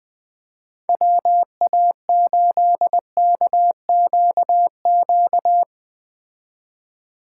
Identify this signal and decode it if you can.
Morse code, heard as a keyed tone, about 20 wpm, 700 Hz. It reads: WA8KQQ
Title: Speed Racing - USA Call Signs 17wpm